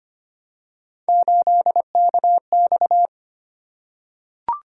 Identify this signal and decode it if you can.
Morse code, heard as a keyed tone, about 25 wpm, 700 Hz. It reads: 8KX
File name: Three Letter-Number Combinations - 25wpm